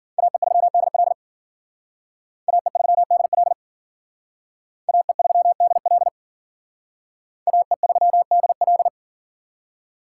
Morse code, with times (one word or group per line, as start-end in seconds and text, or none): 0.2-1.1 AE3DL
2.5-3.5 AE3DL
4.9-6.1 AE3DL
7.5-8.9 AE3DL